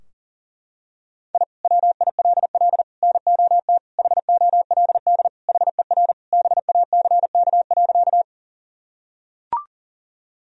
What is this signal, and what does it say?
Morse code, heard as a keyed tone, about 37 wpm, 700 Hz. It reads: I WILL NOT HOLD HER BACK.
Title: Sentences from Top 1000 Words - Review Part 2 - 40wpm